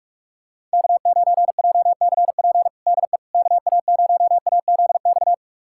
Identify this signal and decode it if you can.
Morse code, heard as a keyed tone, about 42 wpm, 690 Hz. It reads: K9JCP DE KA0AZX
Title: USA Callsign DE Callsign 45wpm